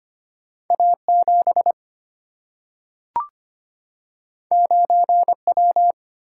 Morse code, heard as a keyed tone, about 25 wpm, 700 Hz. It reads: A 7 9 W
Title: Single-Letter Single-Number 25wpm